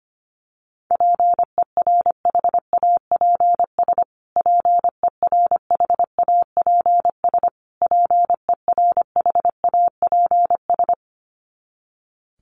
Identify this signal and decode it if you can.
Morse code, heard as a keyed tone, about 25 wpm, 700 Hz. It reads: PERHAPS PERHAPS PERHAPS
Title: Top 500 Words Repeated 25wpm